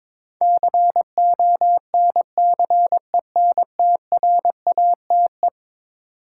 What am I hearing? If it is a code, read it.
Morse code, heard as a keyed tone, about 22 wpm, 700 Hz. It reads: CONCENTRATE
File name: practice / 22wpm / Eleven Letter Words - 22wpm